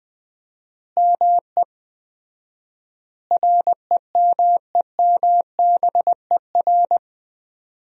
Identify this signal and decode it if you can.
Morse code, heard as a keyed tone, about 20 wpm, 700 Hz. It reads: ME REMEMBER
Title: 4x Word-Spacing - Sentences from Top 400 Words - 20wpm